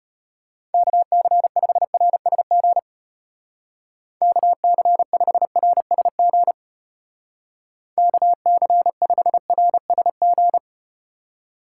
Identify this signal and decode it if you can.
Morse code, heard as a keyed tone, about 34 wpm, 700 Hz. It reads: KC5RSG KC5RSG KC5RSG